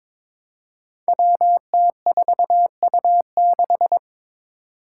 Morse code, heard as a keyed tone, about 23 wpm, 700 Hz. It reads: WT4U6
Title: Five Challenging Letter-Number Combinations 22wpm